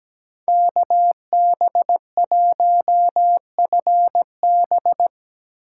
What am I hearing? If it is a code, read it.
Morse code, heard as a keyed tone, about 17 wpm, 700 Hz. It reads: KB1FB
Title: USA Call Signs - 5 Characters - 4x Character Spacing - 17wpm